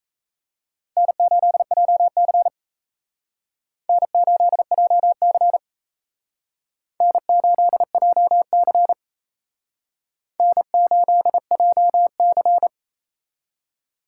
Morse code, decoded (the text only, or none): N8JC N8JC N8JC N8JC